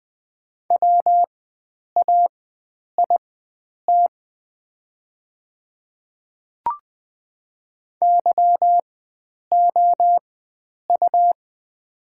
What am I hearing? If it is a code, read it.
Morse code, heard as a keyed tone, about 20 wpm, spaced as for 9 wpm, 690 Hz. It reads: WAIT YOU
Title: 4x Character Spacing - Top 500 Words - 20wpm